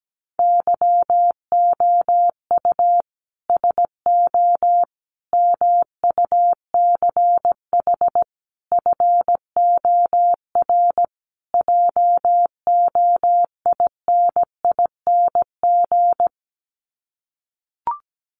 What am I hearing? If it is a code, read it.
Morse code, heard as a keyed tone, about 17 wpm, 700 Hz. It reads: YOU SO MUCH FOR JOINING
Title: Sets of 5 Words Encore2 17wpm